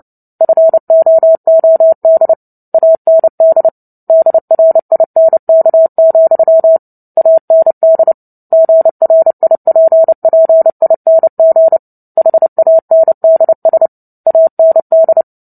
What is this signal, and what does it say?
Morse code, heard as a keyed tone, about 30 wpm, 650 Hz. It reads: FOOD AND DRINK, AND GRIPPING HANDS AND
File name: Book Two / 03-Chapter 3 - 29 WPM